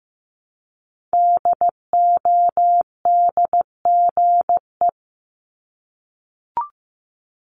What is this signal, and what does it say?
Morse code, heard as a keyed tone, about 15 wpm, 700 Hz. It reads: DODGE
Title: Multi-letter Phonemes Repeated 3x 15wpm